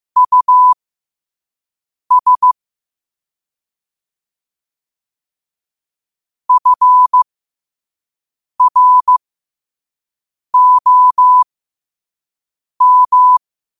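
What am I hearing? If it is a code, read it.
Morse code, heard as a keyed tone, about 15 wpm, spaced as for 5 wpm, 1000 Hz. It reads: US FROM
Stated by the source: Morse code quote of the day sent at 5 wpm.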